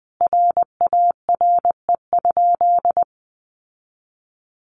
Morse code, heard as a keyed tone, about 20 wpm, 700 Hz. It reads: RARE?
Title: Sentences from Top 1100 Words 20wpm